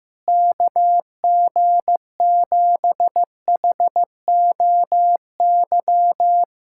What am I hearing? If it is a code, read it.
Morse code, heard as a keyed tone, about 16 wpm, 700 Hz. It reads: KG7HOY
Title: USA Callsign DE Callsign 15wpm